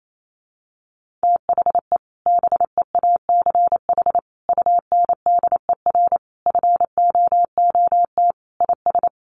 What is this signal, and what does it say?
Morse code, heard as a keyed tone, about 28 wpm, 700 Hz. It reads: THE BEACH UNDER FOOT IS
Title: Sentences from Top 1500 Words 28wpm